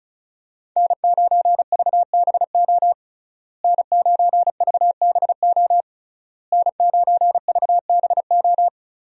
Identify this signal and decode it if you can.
Morse code, heard as a keyed tone, about 36 wpm, 700 Hz. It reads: N9VBO N9VBO N9VBO
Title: USA Call Signs - 5 Characters - Repeated 3x - 35wpm